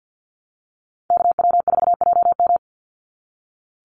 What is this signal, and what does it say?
Morse code, heard as a keyed tone, about 50 wpm, 700 Hz. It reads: KW4JM
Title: NA SOTA Activator Callsigns 50wpm